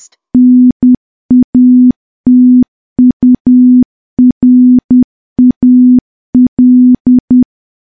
Morse code, heard as a keyed tone, about 10 wpm, 250 Hz. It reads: NATURAL
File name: morse / morse-training-10WPM